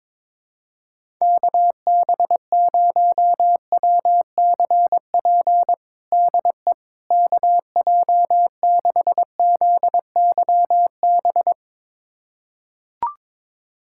Morse code, heard as a keyed tone, about 22 wpm, 700 Hz. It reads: KB0WCP DE KJ6ZYB
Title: USA Callsign DE Callsign - 6 Characters - 22wpm